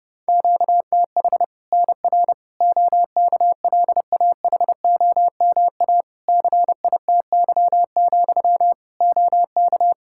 Morse code, heard as a keyed tone, about 30 wpm, 700 Hz. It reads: QTH NR OKLAHOMA CITY, OK
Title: Practice QSO - Exchange 1 - Beginning - 30wpm